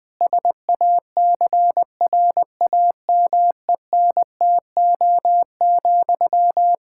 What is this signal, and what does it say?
Morse code, heard as a keyed tone, about 20 wpm, 700 Hz. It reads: SACRAMENTO,
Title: K1USN SST QSO Practice 20wpm